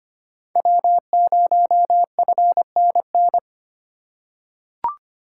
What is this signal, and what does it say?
Morse code, heard as a keyed tone, about 25 wpm, 700 Hz. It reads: W0FNN
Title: USA Call Signs - 5 Characters - 2x Character Spacing - 25wpm